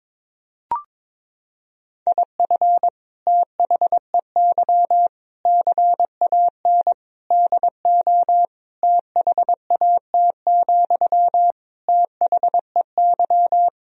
Morse code, heard as a keyed tone, about 21 wpm, 700 Hz. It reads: IF THEY CAN DO THAT, THEY
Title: Sentences from Top 1000 Words - Review Part 4 - 22wpm